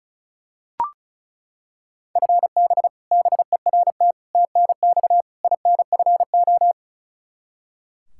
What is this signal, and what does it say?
Morse code, heard as a keyed tone, about 34 wpm, 700 Hz. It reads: FB BERT TNX INFO
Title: Practice QSO - Exchange 3 - More Info - 35wpm